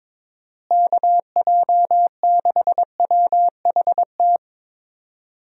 Morse code, heard as a keyed tone, about 22 wpm, 700 Hz. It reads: KJ6WHT
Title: USA Call Signs - 6 Characters - Repeated 3x - 22wpm